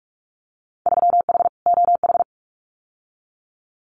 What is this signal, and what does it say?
Morse code, heard as a keyed tone, about 46 wpm, 690 Hz. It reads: 3H OH
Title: Winter Field Day Exchanges Encore 45wpm